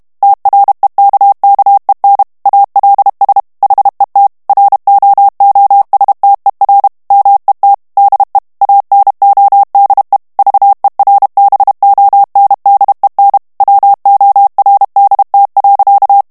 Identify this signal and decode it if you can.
Morse code, heard as a keyed tone, about 32 wpm, 800 Hz. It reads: TREKKEN ALS HET ROOSTER MET DE ANODE VERBONDEN WORDT.
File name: HOLLANDAIS texte clair 2 vitesse 30 mots minute